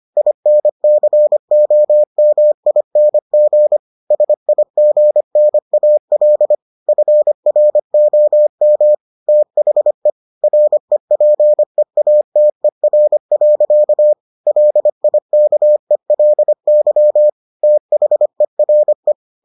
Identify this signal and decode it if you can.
Morse code, heard as a keyed tone, about 25 wpm, 600 Hz. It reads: INCOMING SIGNAL FROM THE REPEATER. LIKELY THERE